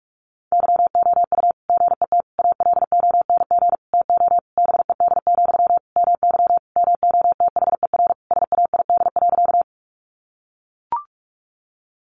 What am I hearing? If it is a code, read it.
Morse code, heard as a keyed tone, about 44 wpm, 700 Hz. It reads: YOU GET ALONG TO BED, MY MOTHER SAID.
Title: Sentences from Top 700 Words 45wpm